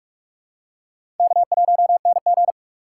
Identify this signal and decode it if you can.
Morse code, heard as a keyed tone, about 43 wpm, 700 Hz. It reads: K1NG
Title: USA Call Signs - 4x Character Spacing - 45wpm